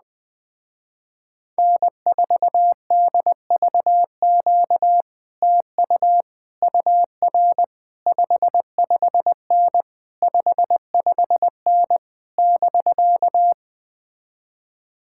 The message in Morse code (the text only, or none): N4DVQ TU UR 55N 55N <BK>